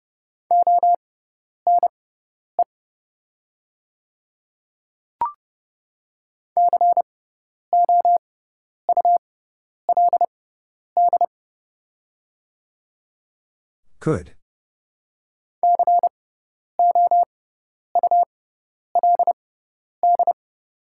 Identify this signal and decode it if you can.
Morse code, heard as a keyed tone, about 29 wpm, spaced as for 10 wpm, 690 Hz. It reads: ONE COULD COULD
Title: 6x Character Spacing - Top 100 Words - 30wpm